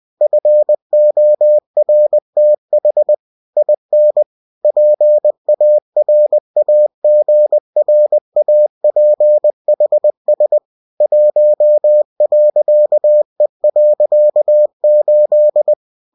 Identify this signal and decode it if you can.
Morse code, heard as a keyed tone, about 20 wpm, 600 Hz. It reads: FORTH IN PARAGRAPHS 1.E.8
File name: cw-books / winnie-the-pooh / winnie-20wpm-page-0115